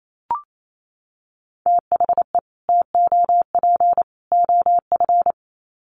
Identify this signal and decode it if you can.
Morse code, heard as a keyed tone, about 28 wpm, 700 Hz. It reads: THE TOP OF
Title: Sets of 3 Words 28wpm